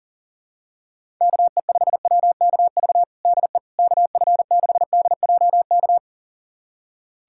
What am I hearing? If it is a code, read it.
Morse code, heard as a keyed tone, about 40 wpm, 700 Hz. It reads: KE5WKV DE KF6DJK
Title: USA Callsign DE Callsign - 6 Characters - 40wpm